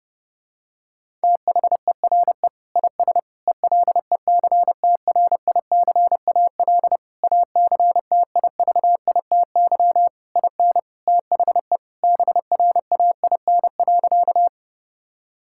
Morse code, heard as a keyed tone, about 30 wpm, 700 Hz. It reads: THERE IS ELECTRICAL ACTIVITY IN THE BRAIN.
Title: Sentences from Top 1000 Words - Review Part 6 - 30wpm